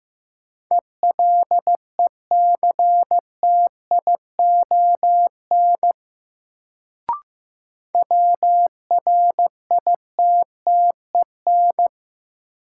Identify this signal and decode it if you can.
Morse code, heard as a keyed tone, about 15 wpm, 700 Hz. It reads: ELECTION WRITTEN
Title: Sentences from Top 1500 Words - New Word Drill Alt - 15wpm